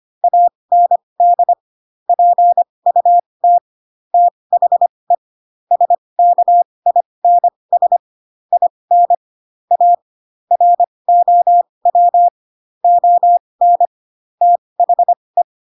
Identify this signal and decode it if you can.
Morse code, heard as a keyed tone, about 25 wpm, spaced as for 20 wpm, 700 Hz. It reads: AND PUT THE SKINS IN A ROW ON THE
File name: Pinocchio / Pinocchio ch7 15 25 ik1hge